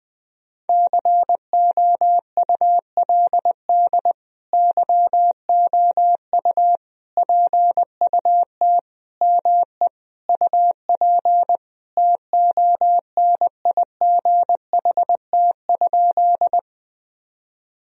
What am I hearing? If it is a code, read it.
Morse code, heard as a keyed tone, about 20 wpm, 700 Hz. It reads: COULD YOU PUT ME UP TONIGHT?
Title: Sentences from Top 1000 Words - Review Part 3 - 20wpm